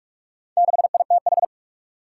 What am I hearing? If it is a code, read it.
Morse code, heard as a keyed tone, about 46 wpm, 700 Hz. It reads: 6ITH
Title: Four Challenging Letter-Number Combinations 45wpm